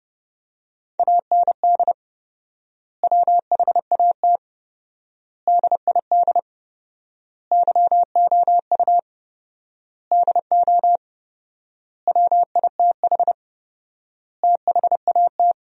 Morse code, heard as a keyed tone, about 30 wpm, 700 Hz. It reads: AND WHAT DID YOU DO WITH THAT
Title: 4x Word-Spacing - Sentences from Top 300 Words - 30wpm